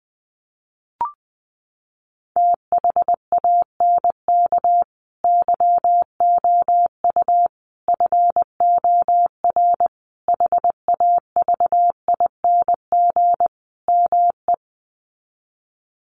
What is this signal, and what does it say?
Morse code, heard as a keyed tone, about 20 wpm, 700 Hz. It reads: THANK YOU FOR HAVING ME
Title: Sets of 5 Words Encore 20wpm